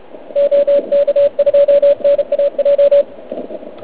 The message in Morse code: OK2NAJ